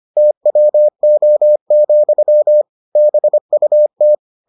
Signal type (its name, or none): Morse code